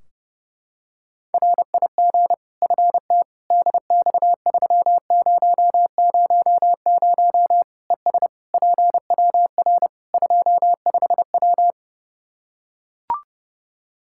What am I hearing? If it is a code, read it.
Morse code, heard as a keyed tone, about 30 wpm, 700 Hz. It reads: RIG FT DX3000 ES PWR 25W
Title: K1USN SST QSO Practice 30wpm